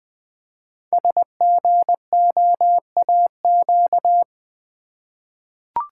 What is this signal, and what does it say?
Morse code, heard as a keyed tone, about 20 wpm, 700 Hz. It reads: SGOAQ
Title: Code Groups - 5 Letters - 20wpm